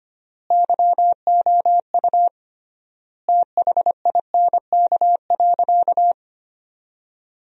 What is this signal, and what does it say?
Morse code, heard as a keyed tone, about 25 wpm, 700 Hz. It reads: YOU THINK.
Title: 3x Word-Spacing - Sentences from Top 400 Words - 25wpm